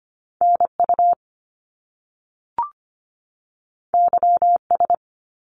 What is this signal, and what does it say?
Morse code, heard as a keyed tone, about 25 wpm, 700 Hz. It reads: NU YS
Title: Top Two Letter Combinations in Words 25wpm